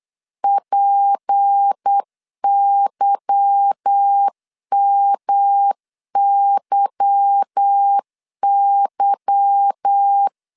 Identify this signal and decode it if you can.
Morse code, heard as a keyed tone, about 8 wpm, 800 Hz. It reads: PYMYY
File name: LEZIONI / 8 YJ45 / 8 Y 8wpm 3245